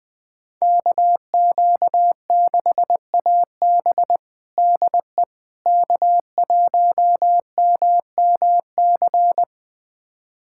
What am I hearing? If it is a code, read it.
Morse code, heard as a keyed tone, about 21 wpm, 700 Hz. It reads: KQ6AB DE K1MMC
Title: USA Callsign DE Callsign - 5 Characters - 20wpm